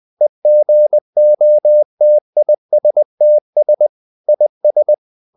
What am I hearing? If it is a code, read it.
Morse code, heard as a keyed tone, about 20 wpm, 600 Hz. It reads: EGOTISTS IS
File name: CW / Practice / EnglishQuotations0320